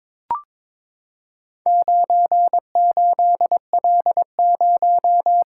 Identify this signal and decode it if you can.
Morse code, heard as a keyed tone, about 22 wpm, 700 Hz. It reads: 98L0
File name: practice / 22wpm / Four Letter-Number Combinations - 22wpm